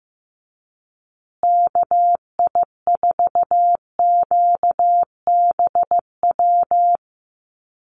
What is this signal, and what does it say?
Morse code, heard as a keyed tone, about 15 wpm, 700 Hz. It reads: KI4QBW
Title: USA Call Signs Repeated 2x 15wpm